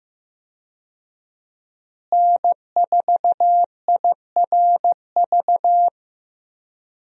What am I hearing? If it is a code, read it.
Morse code, heard as a keyed tone, about 15 wpm, 700 Hz. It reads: N4IRV
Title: USA Call Signs - 5 Characters - Repeated 2x - 15wpm